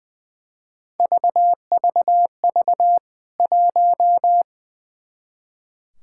Morse code, heard as a keyed tone, about 20 wpm, 700 Hz. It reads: VVV 1
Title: K1USN SST QSO Practice 20wpm